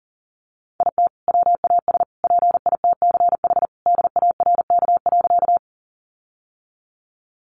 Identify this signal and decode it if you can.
Morse code, heard as a keyed tone, about 40 wpm, 700 Hz. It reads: IT WAS PITCH DARK.
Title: Sentences from Top 1300 Words 40wpm